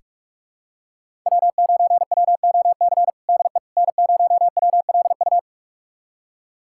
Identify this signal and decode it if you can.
Morse code, heard as a keyed tone, about 46 wpm, 700 Hz. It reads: W9WOC DE N0WLU
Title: USA Callsign DE Callsign - 5 Characters - 45wpm